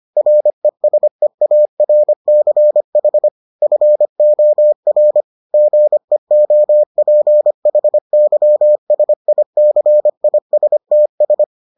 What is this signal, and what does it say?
Morse code, heard as a keyed tone, about 25 wpm, 600 Hz. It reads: RESEARCH FOR GEOPHYSICISTS